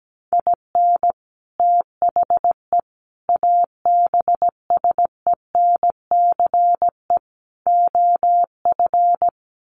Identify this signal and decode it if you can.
Morse code, heard as a keyed tone, about 17 wpm, 700 Hz. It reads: IN THE ABSENCE OF
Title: Sets of 4 Words 17wpm